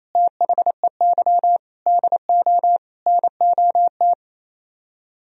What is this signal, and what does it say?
Morse code, heard as a keyed tone, about 34 wpm, 700 Hz. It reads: THEY DO NOT
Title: Speed-Racing - Sets of 4 Words 28wpm